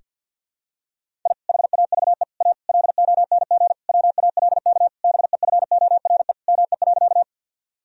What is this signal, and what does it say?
Morse code, heard as a keyed tone, about 52 wpm, 700 Hz. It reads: I HAVE A LONG WALK BEFORE ME.
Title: Sentences from Top 1000 Words - Review Part 1 - 50wpm